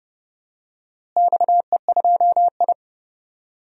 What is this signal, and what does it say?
Morse code, heard as a keyed tone, about 30 wpm, 700 Hz. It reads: XE2I
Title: DXCC-100 Call Signs - 4 Characters - Repeated 2x - 30wpm